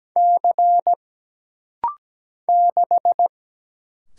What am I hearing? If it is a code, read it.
Morse code, heard as a keyed tone, about 17 wpm, 700 Hz. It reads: C 6
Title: Single Letter-Number Alt - Rapid-Fire 17wpm